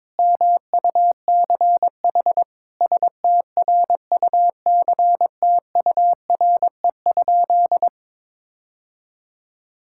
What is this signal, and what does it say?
Morse code, heard as a keyed tone, about 22 wpm, 700 Hz. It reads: MUCH STRUCTURE?
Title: Sentences from Top 1000 Words - Review Part 6 - 22wpm